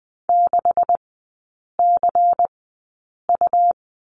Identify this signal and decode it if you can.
Morse code, heard as a keyed tone, about 20 wpm, spaced as for 8 wpm, 690 Hz. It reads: 6CU
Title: 198 Lesson 32e - Letter k - Three Characters 20wpm